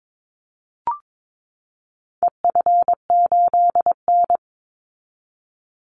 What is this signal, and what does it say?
Morse code, heard as a keyed tone, about 22 wpm, 700 Hz. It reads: EF8N
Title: Call Signs - 22wpm